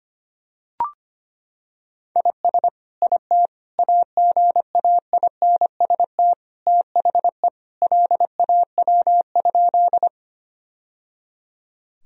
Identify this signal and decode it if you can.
Morse code, heard as a keyed tone, about 25 wpm, 700 Hz. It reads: IS IT AGAINST THE LAW?
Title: Sentences from Top 1000 Words - Review Part 3 - 25wpm